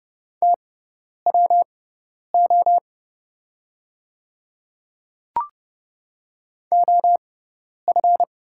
Morse code, heard as a keyed tone, about 30 wpm, spaced as for 10 wpm, 700 Hz. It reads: TWO OF
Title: 6x Character Spacing - Top 100 Words - 30wpm